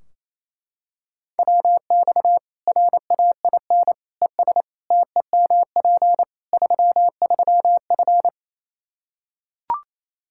Morse code, heard as a keyed tone, about 28 wpm, 700 Hz. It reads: WX RAIN ES TEMP 33F
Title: Practice QSO - Exchange 2 - Info - 28wpm